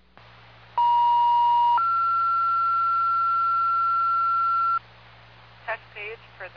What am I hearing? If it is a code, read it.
Two-tone page, heard as a one-second tone, then a three-second tone, about 950 then 1360 Hz, with speech. It is heard from 0.8 to 4.8 s.